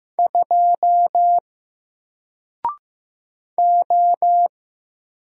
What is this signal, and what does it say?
Morse code, heard as a keyed tone, about 15 wpm, 700 Hz. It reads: 2 O
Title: Single Character (Letter:Number:Punctuation:Prosign) Alt 15wpm